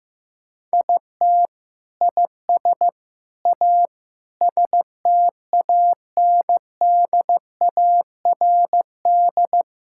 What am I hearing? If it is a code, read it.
Morse code, heard as a keyed tone, about 15 wpm, 700 Hz. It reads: IT IS A STANDARD
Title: Sentences from Top 1000 Words - Review Part 6 - 15wpm